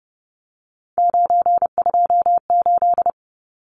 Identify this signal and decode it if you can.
Morse code, heard as a keyed tone, about 30 wpm, 700 Hz. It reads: 928
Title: Numbers - 3-Digit 30wpm